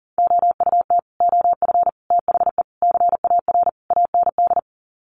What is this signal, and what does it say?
Morse code, heard as a keyed tone, about 40 wpm, 700 Hz. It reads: OUT OF THE CAR AND
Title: Sets of 5 Words Encore 40wpm